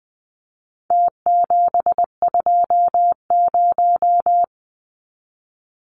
Morse code, heard as a keyed tone, about 20 wpm, 700 Hz. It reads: T720
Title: Numbers - Contest Style - 4-Digit 20wpm